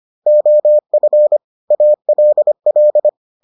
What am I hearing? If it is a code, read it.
Morse code, heard as a keyed tone, about 25 wpm, 600 Hz. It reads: OF ALL